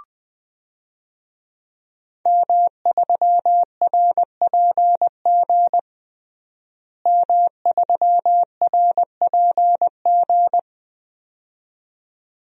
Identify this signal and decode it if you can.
Morse code, heard as a keyed tone, about 20 wpm, 700 Hz. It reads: M3RPG M3RPG
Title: DXCC-100 Call Signs - 5 Characters - Repeated 2x - 20wpm